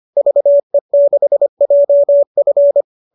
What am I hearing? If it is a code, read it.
Morse code, heard as a keyed tone, about 25 wpm, 570 Hz. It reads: VE6JF
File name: qso203 25